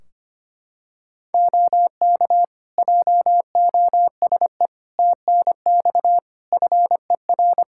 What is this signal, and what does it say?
Morse code, heard as a keyed tone, about 25 wpm, 700 Hz. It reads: OK JOSE TNX FER
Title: K1USN SST QSO Practice 25wpm